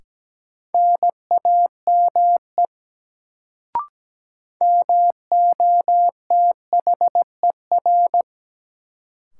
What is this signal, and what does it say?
Morse code, heard as a keyed tone, about 17 wpm, 700 Hz. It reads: NAME MOTHER
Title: Sentences from Top 200 Words - New Word Drill - 17wpm